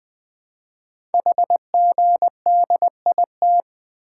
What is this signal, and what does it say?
Morse code, heard as a keyed tone, about 20 wpm, 700 Hz. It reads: HGDIT
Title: Code Groups - 5 Letters - 20wpm